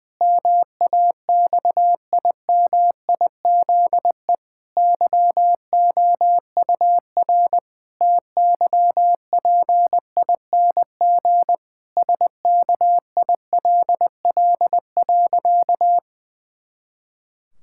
Morse code, heard as a keyed tone, about 21 wpm, 700 Hz. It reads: MAXIMIZE YOUR TYPING SKILL.
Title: Pangram Challenge v2 - 20wpm